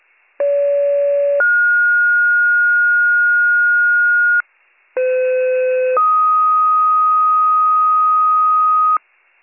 Two-tone page, heard as a one-second tone, then a three-second tone, about 570 then 1430 Hz, about 510 then 1250 Hz, with noise. It is heard from 0.4 to 9.0 s.